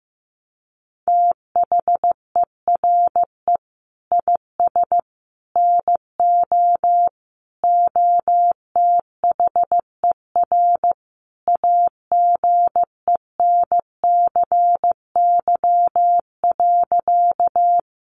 Morse code, heard as a keyed tone, about 15 wpm, 700 Hz. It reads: THERE IS NO OTHER AGENCY.
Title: Sentences from Top 1300 Words 15wpm